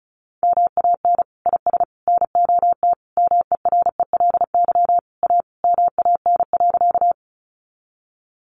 Morse code, heard as a keyed tone, about 34 wpm, 700 Hz. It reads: MAN IS NOT MERELY A MAN.
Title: Sentences from Top 1400 Words 35wpm